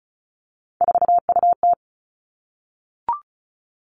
Morse code, heard as a keyed tone, about 36 wpm, 700 Hz. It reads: 4UT
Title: Three Challenging Letter-Number Combinations 35wpm